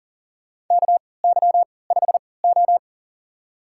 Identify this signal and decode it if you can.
Morse code, heard as a keyed tone, about 40 wpm, spaced as for 23 wpm, 700 Hz. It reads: KY5O
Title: USA Call Signs - 3x Character Spacing - 40wpm